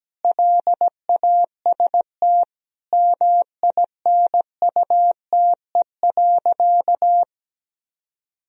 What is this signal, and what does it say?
Morse code, heard as a keyed tone, about 17 wpm, 700 Hz. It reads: LAST MINUTE.
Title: Sentences from Top 1000 Words - Review Part 2 - 17wpm